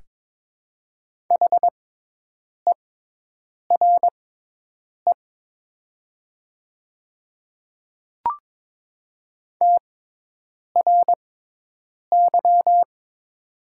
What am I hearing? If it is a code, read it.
Morse code, heard as a keyed tone, about 22 wpm, spaced as for 8 wpm, 700 Hz. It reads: HERE TRY
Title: 6x Character Spacing - Top 500 Words - 22wpm